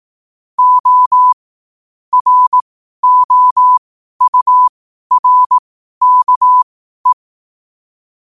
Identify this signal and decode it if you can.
Morse code, heard as a keyed tone, about 18 wpm, spaced as for 12 wpm, 1000 Hz. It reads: O ROURKE
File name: qotdfiles / 12wpm